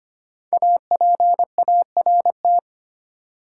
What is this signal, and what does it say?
Morse code, heard as a keyed tone, about 25 wpm, 700 Hz. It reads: APART
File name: 5 Letter Words - First 3 Letters Given - 25wpm